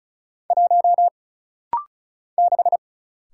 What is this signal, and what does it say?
Morse code, heard as a keyed tone, about 36 wpm, 700 Hz. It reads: 1 6
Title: Single Letter-Number - Rapid-Fire 35wpm